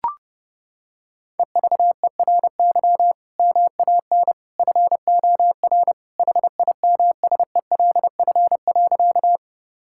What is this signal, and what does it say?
Morse code, heard as a keyed tone, about 30 wpm, 700 Hz. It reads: EVERY MAN FOR HIMSELF.
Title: Sentences from Top 1600 Words 30wpm